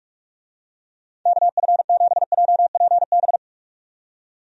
Morse code, heard as a keyed tone, about 41 wpm, 700 Hz. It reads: KF7JPB
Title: USA Call Signs - 6 Characters - 45wpm